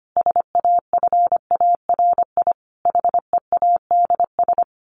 Morse code, heard as a keyed tone, about 25 wpm, 700 Hz. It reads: SAFARI HEADS
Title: A Sound Of Thunder 25wpm